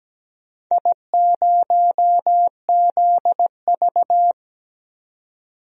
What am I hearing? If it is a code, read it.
Morse code, heard as a keyed tone, about 17 wpm, 700 Hz. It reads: I0ZV
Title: DXCC-100 Call Signs - 4 Characters - 17wpm